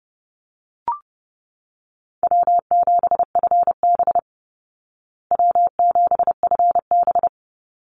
Morse code, heard as a keyed tone, about 30 wpm, 700 Hz. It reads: W7FB W7FB
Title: Easy Call Signs Repeated 2x - 30wpm